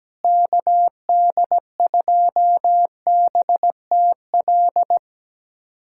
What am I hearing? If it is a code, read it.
Morse code, heard as a keyed tone, about 17 wpm, 700 Hz. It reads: KD2BTL
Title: USA Callsign DE Callsign - 6 Characters - 17wpm